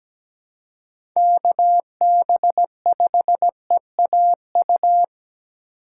Morse code, heard as a keyed tone, about 17 wpm, 700 Hz. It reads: KB5EAU
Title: USA Call Signs - 6 Characters - 2x Character Spacing - 17wpm